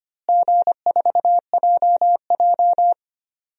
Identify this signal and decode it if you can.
Morse code, heard as a keyed tone, about 25 wpm, 700 Hz. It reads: G4JJ
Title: DXCC-100 Call Signs - 4 Characters - Repeated 3x - 25wpm